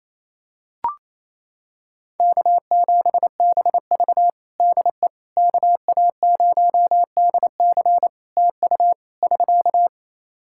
Morse code, heard as a keyed tone, about 28 wpm, 700 Hz. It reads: K7BV DE KA0DC TU <SK>
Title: Practice QSO - Exchange 4 - Ending - 28wpm